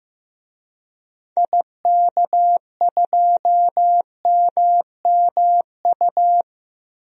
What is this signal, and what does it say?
Morse code, heard as a keyed tone, about 15 wpm, 700 Hz. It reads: IK2MMU
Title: DXCC-100 Call Signs - 6 Characters - 15wpm